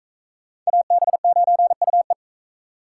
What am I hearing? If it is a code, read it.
Morse code, heard as a keyed tone, about 39 wpm, 700 Hz. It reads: AB9UE